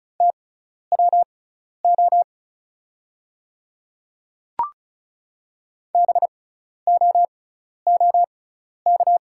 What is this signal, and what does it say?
Morse code, heard as a keyed tone, about 36 wpm, spaced as for 12 wpm, 690 Hz. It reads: TWO BOOK
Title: 6x Character Spacing - Top 300 Words - 35wpm